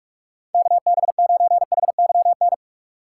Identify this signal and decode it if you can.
Morse code, heard as a keyed tone, about 44 wpm, 700 Hz. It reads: KB9HYN
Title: USA Call Signs - 6 Characters - 2x Character Spacing - 45wpm